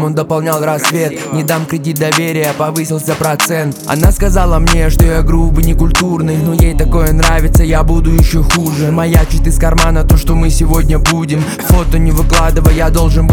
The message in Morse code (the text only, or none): none